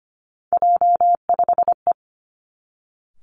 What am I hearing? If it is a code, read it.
Morse code, heard as a keyed tone, about 25 wpm, 700 Hz. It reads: J5E
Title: Three Challenging Letter-Number Combinations 25wpm